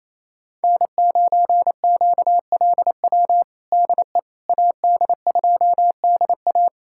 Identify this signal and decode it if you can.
Morse code, heard as a keyed tone, about 28 wpm, 700 Hz. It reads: N9QLW DE AD2DA
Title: USA Callsign DE Callsign - 5 Characters - 28wpm